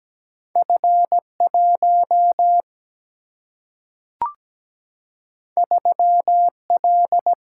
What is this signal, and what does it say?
Morse code, heard as a keyed tone, about 17 wpm, 700 Hz. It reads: F1 3L
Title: Single-Letter Single-Number 17wpm